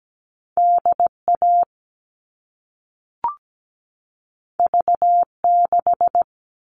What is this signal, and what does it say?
Morse code, heard as a keyed tone, about 17 wpm, 700 Hz. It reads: DA V6
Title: All Cut Numbers - 2-Digit 17wpm